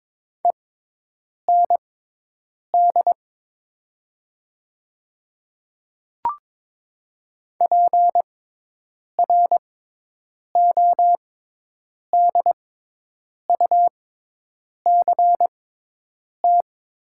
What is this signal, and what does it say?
Morse code, heard as a keyed tone, about 22 wpm, spaced as for 8 wpm, 700 Hz. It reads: END PRODUCT
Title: 6x Character Spacing - Top 500 Words - 22wpm